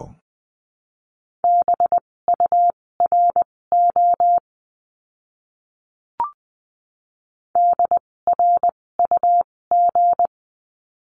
Morse code, heard as a keyed tone, about 20 wpm, spaced as for 16 wpm, 700 Hz. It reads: BURO DRUG
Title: 256 Lesson 42a - Words 12-20wpm